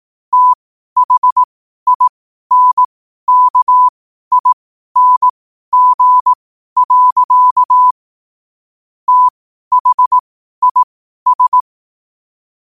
Morse code, heard as a keyed tone, about 18 wpm, spaced as for 12 wpm, 1000 Hz. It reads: THINKING. THIS
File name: qotdfiles / 12wpm